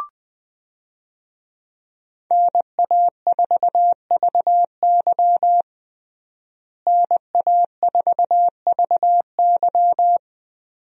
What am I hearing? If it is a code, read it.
Morse code, heard as a keyed tone, about 20 wpm, 700 Hz. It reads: NA4VY NA4VY
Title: USA Call Signs - 5 Characters - Repeated 2x - 20wpm